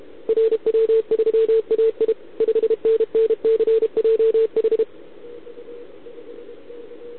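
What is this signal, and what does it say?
Morse code, heard as a keyed tone, about 32 wpm, 440 Hz. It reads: RW3AI 5NN CJH
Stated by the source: Contest CW from OK1RI on 80 m during the OKOM DX contest, recorded by a QRP station using a KX3 and a wire antenna.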